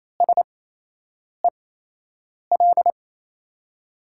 Morse code, heard as a keyed tone, about 27 wpm, spaced as for 8 wpm, 700 Hz. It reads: SEL
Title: USA Call Signs - 8x Character Spacing - 28wpm